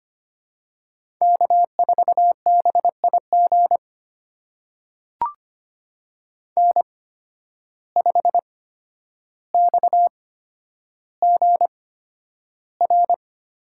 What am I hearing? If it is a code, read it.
Morse code, heard as a keyed tone, about 25 wpm, spaced as for 11 wpm, 700 Hz. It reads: K4BIG N5XGR
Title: USA Call Signs - 5 Characters - 8x Character Spacing - 25wpm